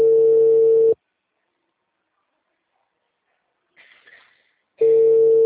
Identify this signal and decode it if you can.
Ringback tone (China, 450 Hz).
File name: out- -2006- - - .5145